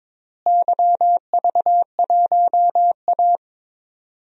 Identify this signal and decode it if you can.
Morse code, heard as a keyed tone, about 22 wpm, 700 Hz. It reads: YV1A
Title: DXCC-100 Call Signs - 4 Characters - Repeated 3x - 22wpm